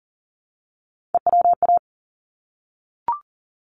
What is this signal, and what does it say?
Morse code, heard as a keyed tone, about 40 wpm, 700 Hz. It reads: EWA
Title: ARRL Sections 40wpm